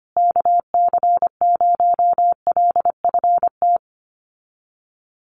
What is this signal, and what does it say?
Morse code, heard as a keyed tone, about 25 wpm, 700 Hz. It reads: KC0LFT
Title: USA Call Signs Repeated 2x 25wpm